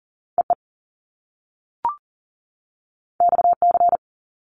Morse code, heard as a keyed tone, about 40 wpm, 700 Hz. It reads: EE XC
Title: Top Two Letter Combinations in Words 40wpm